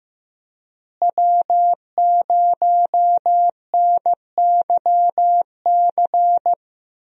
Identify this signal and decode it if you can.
Morse code, heard as a keyed tone, about 15 wpm, 700 Hz. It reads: W0NYC
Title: USA Call Signs - 5 Characters - Repeated 2x - 15wpm